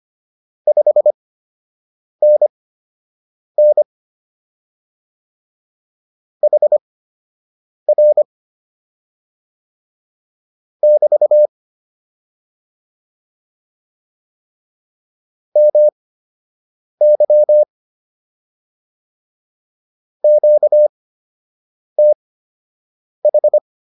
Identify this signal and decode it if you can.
Morse code, heard as a keyed tone, about 25 wpm, spaced as for 7 wpm, 600 Hz. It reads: 5NN HR <BT> MY QTH